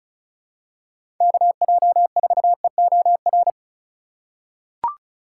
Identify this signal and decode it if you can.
Morse code, heard as a keyed tone, about 36 wpm, 700 Hz. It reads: KJ4EOR
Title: USA Call Signs - 6 Characters - 4x Character Spacing - 35wpm